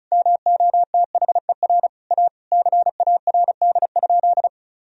Morse code, heard as a keyed tone, about 36 wpm, 700 Hz. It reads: MOTHER A CARD?
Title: Sentences from Top 1000 Words - Review Part 4 - 35wpm